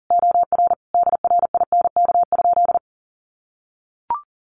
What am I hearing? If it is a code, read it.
Morse code, heard as a keyed tone, about 40 wpm, 700 Hz. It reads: OR DRINK?
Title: Sentences from Top 1500 Words 40wpm